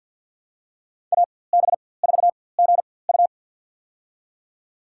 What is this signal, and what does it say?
Morse code, heard as a keyed tone, about 52 wpm, spaced as for 23 wpm, 700 Hz. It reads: AB4GU
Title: USA Call Signs - 4x Character Spacing - 50wpm